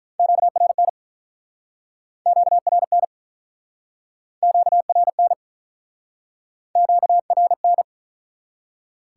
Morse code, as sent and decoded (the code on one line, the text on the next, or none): --.- .-. -. / --.- .-. -. / --.- .-. -. / --.- .-. -.
QRN QRN QRN QRN